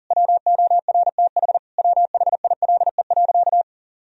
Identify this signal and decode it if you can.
Morse code, heard as a keyed tone, about 40 wpm, 700 Hz. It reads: WORTH WHILE.
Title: Sentences from Top 1000 Words - Review Part 1 - 40wpm